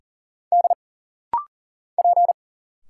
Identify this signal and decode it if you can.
Morse code, heard as a keyed tone, about 40 wpm, 700 Hz. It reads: D P
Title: Single Letters - Rapid-Fire 40wpm